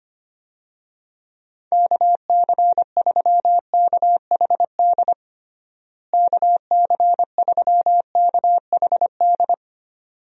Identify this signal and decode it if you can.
Morse code, heard as a keyed tone, about 25 wpm, 700 Hz. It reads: KC3KHD KC3KHD
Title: USA Call Signs - 6 Characters - Repeated 2x - 25wpm